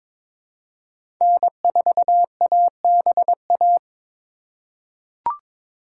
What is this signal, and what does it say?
Morse code, heard as a keyed tone, about 21 wpm, 700 Hz. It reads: N4ABA
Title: USA Call Signs - 5 Characters - 4x Character Spacing - 22wpm